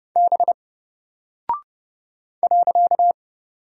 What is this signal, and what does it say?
Morse code, heard as a keyed tone, about 30 wpm, 700 Hz. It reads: B .
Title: Single Character (Letter:Number:Punctuation:Prosign) 30wpm